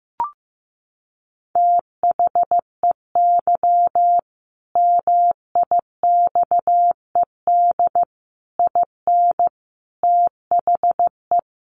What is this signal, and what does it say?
Morse code, heard as a keyed tone, about 15 wpm, 700 Hz. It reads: THEY MIXED IN THE
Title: Sentences from Top 1500 Words 15wpm